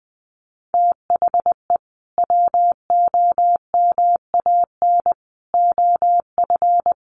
Morse code, heard as a keyed tone, about 20 wpm, 700 Hz. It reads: THE WOMAN OF
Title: Sentences from Top 1500 Words 20wpm